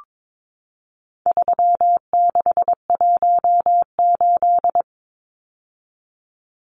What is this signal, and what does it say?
Morse code, heard as a keyed tone, about 22 wpm, 700 Hz. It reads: 3618
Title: Numbers - 4-Digit 22wpm